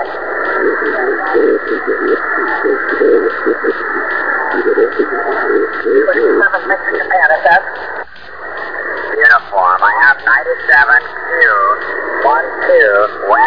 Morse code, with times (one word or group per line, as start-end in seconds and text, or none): none